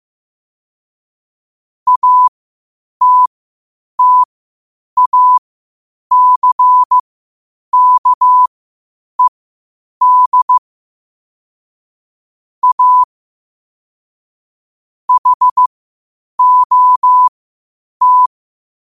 Morse code, heard as a keyed tone, about 16 wpm, spaced as for 8 wpm, 1000 Hz. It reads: ATTACKED A HOT
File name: qotdfiles / 8wpm